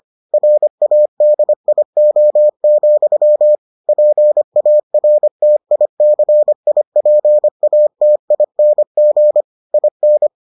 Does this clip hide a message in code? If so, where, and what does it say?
Yes: Morse code from 0.3 to 10.3 s, heard as a keyed tone, about 25 wpm, 600 Hz. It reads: RADIO, PARTICIPATING IN